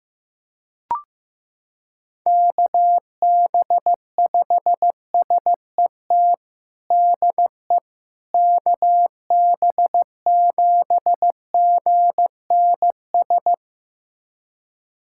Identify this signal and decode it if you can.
Morse code, heard as a keyed tone, about 16 wpm, 700 Hz. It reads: KB5SET DE KB7GNS
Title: USA Callsign DE Callsign - 6 Characters - 15wpm